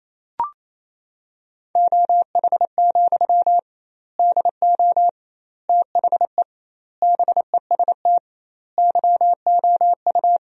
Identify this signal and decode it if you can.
Morse code, heard as a keyed tone, about 28 wpm, 700 Hz. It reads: OH, DO THE BEST YOU
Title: 2x Word-Spacing - Sentences from Top 500 Words - 28wpm